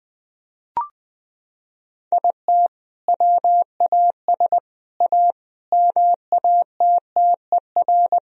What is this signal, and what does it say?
Morse code, heard as a keyed tone, about 20 wpm, 700 Hz. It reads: IT WAS A MATTER
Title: Sentences from Top 1000 Words - Review Part 4 - 20wpm